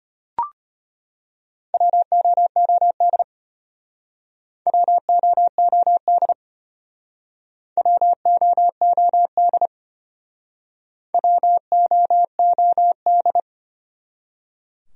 Morse code, heard as a keyed tone, about 33 wpm, 700 Hz. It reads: WOOD WOOD WOOD WOOD